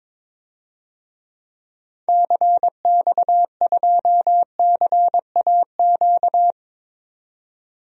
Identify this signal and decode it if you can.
Morse code, heard as a keyed tone, about 21 wpm, 700 Hz. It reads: CX2CAQ
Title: DXCC-100 Call Signs - 6 Characters - 22wpm